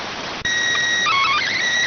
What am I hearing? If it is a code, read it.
SSTV (slow-scan television), PD 290.